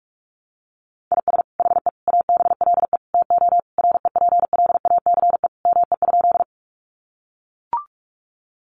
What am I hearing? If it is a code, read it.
Morse code, heard as a keyed tone, about 44 wpm, 700 Hz. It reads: IS HE ABLE TO REPLACE ME?
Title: Sentences from Top 1400 Words 45wpm